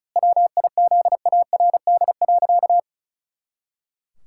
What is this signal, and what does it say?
Morse code, heard as a keyed tone, about 36 wpm, 700 Hz. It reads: WIZARD.
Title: Pangram Challenge v2 - 35wpm